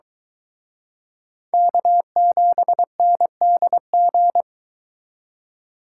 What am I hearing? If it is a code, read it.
Morse code, heard as a keyed tone, about 22 wpm, 700 Hz. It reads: K7NDG